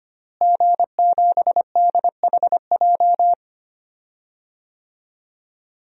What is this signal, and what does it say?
Morse code, heard as a keyed tone, about 25 wpm, 700 Hz. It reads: G7DHJ
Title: DXCC-100 Call Signs - 5 Characters - Repeated 3x - 25wpm